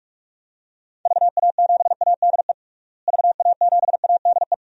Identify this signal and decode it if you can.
Morse code, heard as a keyed tone, about 46 wpm, 700 Hz. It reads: VA7ADE VA7ADE
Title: DXCC-100 Call Signs - 6 Characters - Repeated 2x - 45wpm